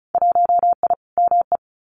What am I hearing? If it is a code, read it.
Morse code, heard as a keyed tone, about 34 wpm, 700 Hz. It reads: 1I ME
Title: Winter Field Day Exchanges Encore 35wpm